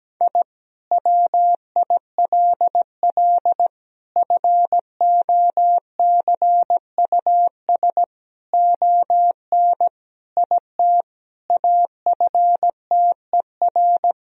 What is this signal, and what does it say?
Morse code, heard as a keyed tone, about 17 wpm, 700 Hz. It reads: I WILL FOCUS ON IT AFTER
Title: Sentences from Top 1000 Words - Review Part 4 - 17wpm